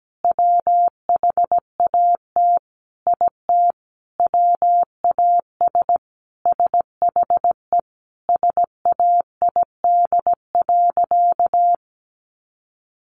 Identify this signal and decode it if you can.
Morse code, heard as a keyed tone, about 17 wpm, 700 Hz. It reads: WHAT IT WAS SHE SAID.
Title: Sentences from Top 400 Words 17wpm